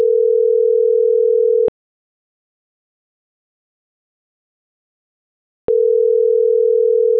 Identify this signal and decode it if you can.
Ringback tone (North America), 440 + 480 Hz.